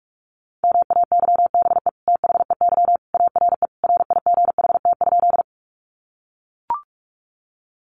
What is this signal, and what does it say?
Morse code, heard as a keyed tone, about 44 wpm, 700 Hz. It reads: MAYBE THEY ARE RIGHT?
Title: Sentences from Top 1000 Words 45wpm